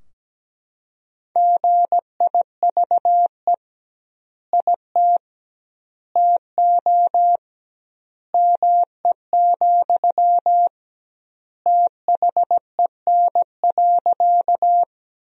Morse code, heard as a keyed tone, about 17 wpm, 700 Hz. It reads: GIVE IT TO ME, THEN.
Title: 2x Word-Spacing - Sentences from Top 200 Words - 17wpm